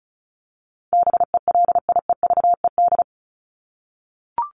Morse code, heard as a keyed tone, about 34 wpm, 700 Hz. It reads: BELIEVED
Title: ED Suffix - Top 100 Words with Base Words 35wpm